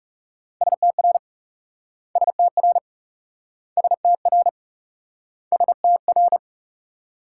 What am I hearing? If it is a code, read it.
Morse code, heard as a keyed tone, about 38 wpm, 700 Hz. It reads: STR STR STR STR